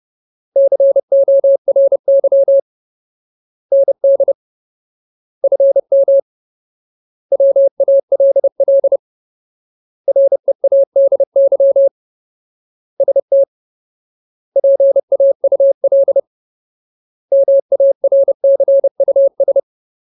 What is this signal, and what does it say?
Morse code, heard as a keyed tone, about 30 wpm, 550 Hz. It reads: CORY ND FM WALL READY ST PAUL MARCUS
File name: Ultimate Word List / 30WPM Ultimate